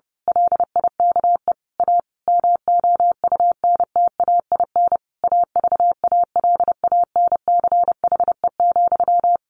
Morse code, heard as a keyed tone, about 30 wpm, 700 Hz. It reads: LIKE A MOUNTAIN AVALANCHE,
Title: A Sound Of Thunder 30wpm